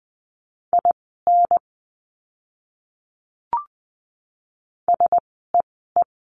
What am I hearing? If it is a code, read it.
Morse code, heard as a keyed tone, about 20 wpm, spaced as for 14 wpm, 700 Hz. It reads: IN SEE
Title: 244 Lesson 39f - Letter z - Words 20wpm